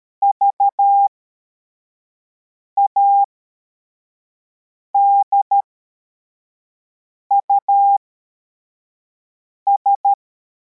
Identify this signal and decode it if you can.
Morse code, heard as a keyed tone, about 13 wpm, spaced as for 4 wpm, 800 Hz. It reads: VADUS
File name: formation / cw / Text52